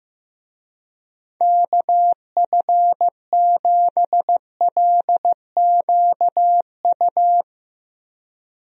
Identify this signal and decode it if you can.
Morse code, heard as a keyed tone, about 16 wpm, 700 Hz. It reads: KF7LQU
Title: USA Call Signs - 6 Characters - Repeated 2x - 15wpm